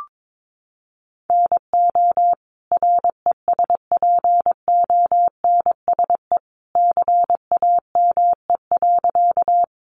Morse code, heard as a keyed tone, about 22 wpm, 700 Hz. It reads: NO RESPONSE CAME.
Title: Sentences from Top 1400 Words 22wpm